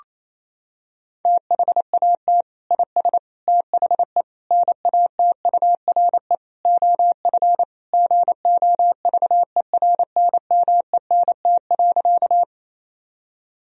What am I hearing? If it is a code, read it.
Morse code, heard as a keyed tone, about 28 wpm, 700 Hz. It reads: THAT IS THE NATURE OF GOVERNMENT.
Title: Sentences from Top 1000 Words - Review Part 3 - 28wpm